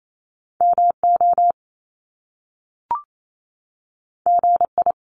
Morse code, heard as a keyed tone, about 28 wpm, 700 Hz. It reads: MO GI
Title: Top Two Letter Combinations in Words 28wpm